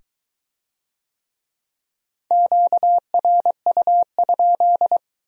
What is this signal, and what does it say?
Morse code, heard as a keyed tone, about 20 wpm, 700 Hz. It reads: QRU?